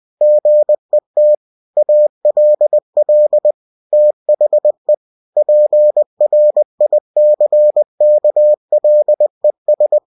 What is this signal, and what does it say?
Morse code, heard as a keyed tone, about 20 wpm, 600 Hz. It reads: GET ALL THE PRICKLES
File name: winnie-20wpm